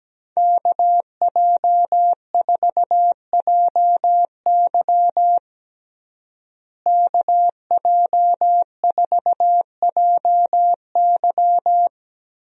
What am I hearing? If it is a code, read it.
Morse code, heard as a keyed tone, about 17 wpm, 700 Hz. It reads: KJ4JY KJ4JY
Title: USA Call Signs - 5 Characters - Repeated 2x - 17wpm